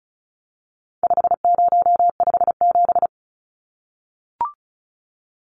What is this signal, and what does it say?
Morse code, heard as a keyed tone, about 34 wpm, 700 Hz. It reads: 5057
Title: Numbers - 4-Digit 35wpm